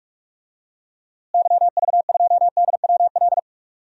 Morse code, heard as a keyed tone, about 46 wpm, 700 Hz. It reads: YV2DWL
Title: DXCC-100 Call Signs - 6 Characters - Repeated 3x - 45wpm